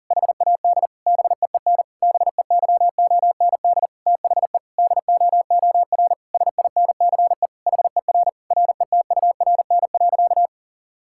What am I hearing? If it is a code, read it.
Morse code, heard as a keyed tone, about 40 wpm, 700 Hz. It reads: HAD BEEN BEYOND THE DOOR SINCE HER RETURN.
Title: Sentences from Top 1000 Words - Review Part 6 - 40wpm